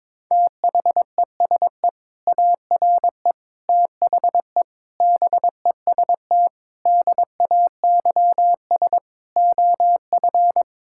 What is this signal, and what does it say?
Morse code, heard as a keyed tone, about 22 wpm, 700 Hz. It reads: THESE ARE THE BEST DAYS OF
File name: practice / 22wpm / Sentences with 2-Word Phrases - 22wpm